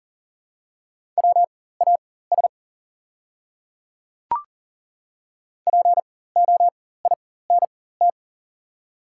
Morse code, heard as a keyed tone, about 40 wpm, spaced as for 19 wpm, 700 Hz. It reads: WAS POINT
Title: 4x Character Spacing - Top 200 Words - 40wpm